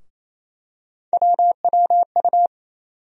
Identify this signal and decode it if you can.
Morse code, heard as a keyed tone, about 28 wpm, 700 Hz. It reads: WWU
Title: USA Call Sign Postfixes - 3 Letters 28wpm